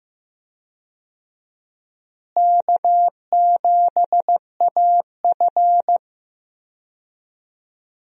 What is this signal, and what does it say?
Morse code, heard as a keyed tone, about 15 wpm, 700 Hz. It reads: K7AF
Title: USA Call Signs - 4 Characters - Repeated 3x - 15wpm